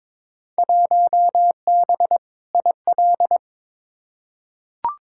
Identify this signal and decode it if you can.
Morse code, heard as a keyed tone, about 22 wpm, 700 Hz. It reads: 1B IL
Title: Field Day Exchanges - 22wpm